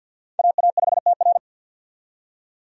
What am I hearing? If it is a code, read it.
Morse code, heard as a keyed tone, about 53 wpm, 700 Hz. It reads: AA5TR
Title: USA Call Signs - 5 Characters - 50wpm